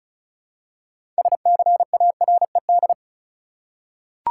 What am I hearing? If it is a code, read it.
Morse code, heard as a keyed tone, about 36 wpm, 700 Hz. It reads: SCARED
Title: Six Letter Words - 35wpm